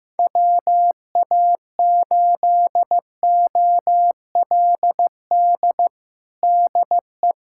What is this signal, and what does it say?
Morse code, heard as a keyed tone, about 15 wpm, 700 Hz. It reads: WA8OLD DE
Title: USA Callsign DE Callsign - 6 Characters - 15wpm